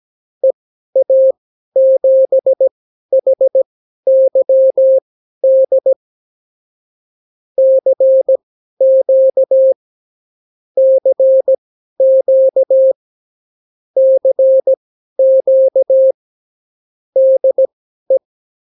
Morse code, heard as a keyed tone, about 17 wpm, spaced as for 12 wpm, 530 Hz. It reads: EA7HYD CQ CQ CQ DE
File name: CW QSO libroCW4 18-8ppm